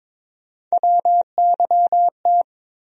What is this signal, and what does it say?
Morse code, heard as a keyed tone, about 21 wpm, 700 Hz. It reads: WYT
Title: USA Call Sign Postfixes - 3 Letters 22wpm